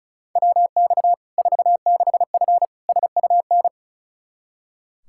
Morse code, heard as a keyed tone, about 34 wpm, 700 Hz. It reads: WX 46F SUN
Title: Practice QSO - Exchange 2 - Info - 35wpm